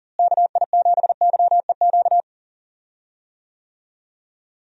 Morse code, heard as a keyed tone, about 37 wpm, 700 Hz. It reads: KI7YEQ
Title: USA Call Signs - 6 Characters - Repeated 3x - 40wpm